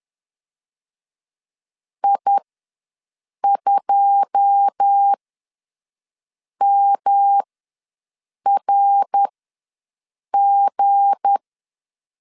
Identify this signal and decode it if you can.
Morse code, heard as a keyed tone, about 10 wpm, spaced as for 5 wpm, 800 Hz. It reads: I2MRG
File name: LEZIONI / 9 2 ws 10wpm 4130